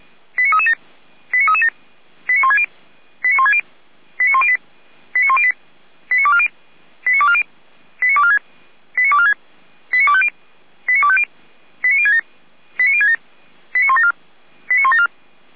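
Five-tone selective call (ZVEI), heard as quick runs of five tones, tones of about 70 ms. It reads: 89337 89337 89160 89160 89108 89108 89240 89240 89246 89246 89250 89250 89067 89067 89174 89174
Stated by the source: ZVEI five-tone selective call received with an SDR.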